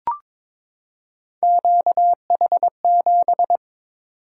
Morse code, heard as a keyed tone, about 22 wpm, 700 Hz. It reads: QH7
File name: practice / 22wpm / Three Letter-Number Combinations - 22wpm